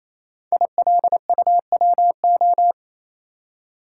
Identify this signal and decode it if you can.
Morse code, heard as a keyed tone, about 28 wpm, 700 Hz. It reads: ILUWO
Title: Code Groups - 5 Letters - 28wpm